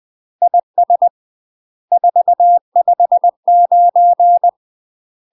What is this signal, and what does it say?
Morse code, heard as a keyed tone, about 20 wpm, 700 Hz. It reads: IS 459
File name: UK MASTER QSOs TRAINING / 20WPM / UK MASTER QSOs 0064